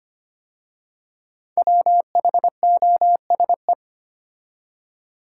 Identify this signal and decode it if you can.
Morse code, heard as a keyed tone, about 25 wpm, 700 Hz. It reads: WHOSE
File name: 5 Letter Words - First 3 Letters Given - 25wpm